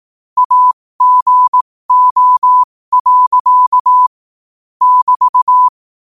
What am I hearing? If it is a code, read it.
Morse code, heard as a keyed tone, about 18 wpm, spaced as for 15 wpm, 1000 Hz. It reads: AGO. <BT>
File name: qotdfiles / 15wpm